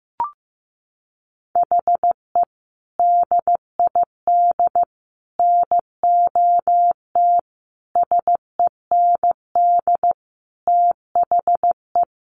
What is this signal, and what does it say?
Morse code, heard as a keyed tone, about 15 wpm, 700 Hz. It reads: HE DID NOT SEND THE
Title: Sentences from Top 1500 Words 15wpm